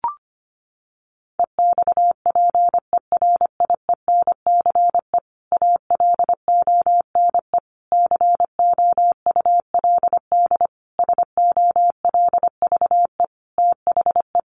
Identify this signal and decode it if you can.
Morse code, heard as a keyed tone, about 25 wpm, 700 Hz. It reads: EXPERIENCE ALONE COULD SOLVE THE
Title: Sentences from Top 1000 Words 25wpm